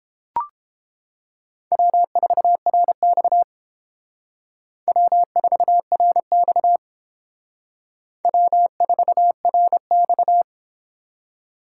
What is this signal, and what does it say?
Morse code, heard as a keyed tone, about 30 wpm, 700 Hz. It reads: W4RX W4RX W4RX